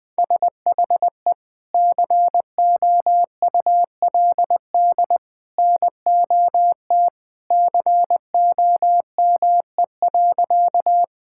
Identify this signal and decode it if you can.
Morse code, heard as a keyed tone, about 20 wpm, 700 Hz. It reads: SHE COULD NOT COME.
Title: Sentences with 2-Word Phrases - 20wpm